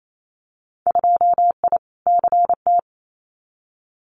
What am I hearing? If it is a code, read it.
Morse code, heard as a keyed tone, about 28 wpm, 700 Hz. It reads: 2I CT
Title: Winter Field Day Exchanges 28wpm